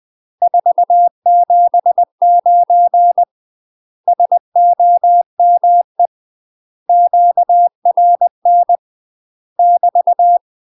Morse code, heard as a keyed tone, about 20 wpm, 700 Hz. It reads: 479 SOME QRN <BT>
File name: UK MASTER QSOs TRAINING / 20WPM / UK MASTER QSOs 0060